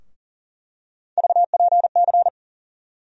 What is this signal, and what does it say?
Morse code, heard as a keyed tone, about 40 wpm, 700 Hz. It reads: VPC
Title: USA Call Sign Postfixes - 3 Letters 40wpm